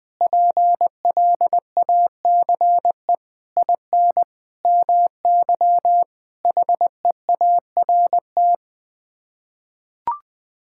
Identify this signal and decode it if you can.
Morse code, heard as a keyed tone, about 21 wpm, 700 Hz. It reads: PLACE IN MY HEART
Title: Sets of 9 Words - 20wpm